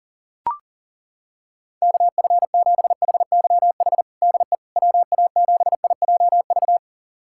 Morse code, heard as a keyed tone, about 40 wpm, 700 Hz. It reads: KF7HYH DE WA7IJV
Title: USA Callsign DE Callsign - 6 Characters - 40wpm